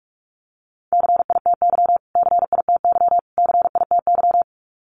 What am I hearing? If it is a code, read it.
Morse code, heard as a keyed tone, about 44 wpm, 700 Hz. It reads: CITY CITY CITY
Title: Top 500 Words Repeated 45wpm